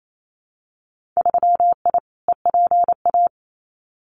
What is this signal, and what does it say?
Morse code, heard as a keyed tone, about 28 wpm, 700 Hz. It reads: 3I EPA
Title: Winter Field Day Exchanges Encore 28wpm